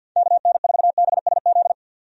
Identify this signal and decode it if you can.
Morse code, heard as a keyed tone, about 50 wpm, 700 Hz. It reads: KN4BSZ
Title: USA Call Signs - 6 Characters - 50wpm